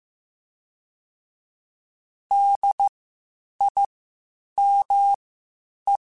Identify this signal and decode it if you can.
Morse code, heard as a keyed tone, about 15 wpm, spaced as for 8 wpm, 800 Hz. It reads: DIME
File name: Spaced Alphas / Spaced Alphas mp3s / Ds sp @13